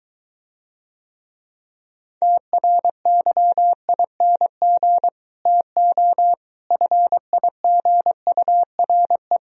Morse code, heard as a keyed tone, about 23 wpm, 700 Hz. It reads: TRYING TO FIGURE